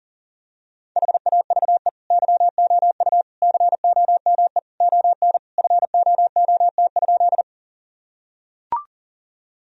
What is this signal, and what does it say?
Morse code, heard as a keyed tone, about 40 wpm, 700 Hz. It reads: HAVE YOU COME ON FOOT?
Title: Sentences from Top 1000 Words - Review Part 3 - 40wpm